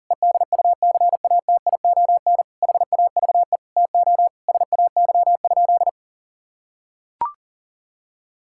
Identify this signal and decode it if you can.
Morse code, heard as a keyed tone, about 37 wpm, 700 Hz. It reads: EDUCATION HAVE TO SAY?
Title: Sentences from Top 1000 Words - Review Part 6 - 40wpm